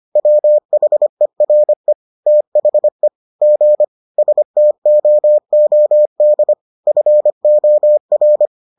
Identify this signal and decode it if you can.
Morse code, heard as a keyed tone, about 25 wpm, 600 Hz. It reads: WHERE THE G STOOD FOR